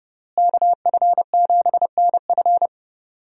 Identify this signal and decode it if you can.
Morse code, heard as a keyed tone, about 30 wpm, 700 Hz. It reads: KF7NF
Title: USA Call Signs - 5 Characters - Repeated 3x - 30wpm